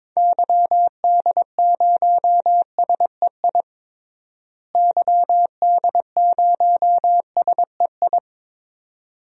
Morse code, heard as a keyed tone, about 22 wpm, 700 Hz. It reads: YD0SEI YD0SEI
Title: DXCC-100 Call Signs - 6 Characters - Repeated 2x - 22wpm